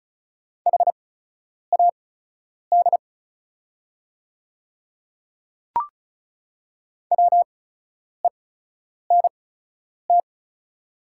Morse code, heard as a keyed tone, about 36 wpm, spaced as for 10 wpm, 690 Hz. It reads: HAD WENT
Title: 8x Character Spacing - Top 200 Words - 35wpm